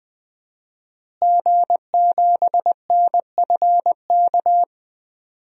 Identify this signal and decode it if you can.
Morse code, heard as a keyed tone, about 20 wpm, 700 Hz. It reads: G7NFK
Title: DXCC-100 Call Signs - 5 Characters - Repeated 3x - 20wpm